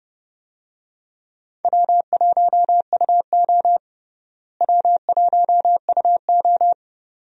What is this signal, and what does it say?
Morse code, heard as a keyed tone, about 30 wpm, 700 Hz. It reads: W1UO W1UO
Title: USA Call Signs - 4 Characters - Repeated 2x - 30wpm